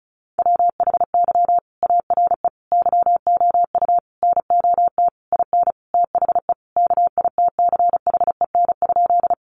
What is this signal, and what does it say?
Morse code, heard as a keyed tone, about 34 wpm, 700 Hz. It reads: WHY ARE YOU NOT IN THE KITCHEN?
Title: Sentences from Top 1400 Words 35wpm